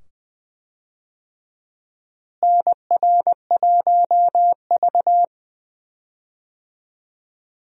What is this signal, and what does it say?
Morse code, heard as a keyed tone, about 20 wpm, 700 Hz. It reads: NR1V
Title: USA Call Signs - 4 Characters - Repeated 2x - 20wpm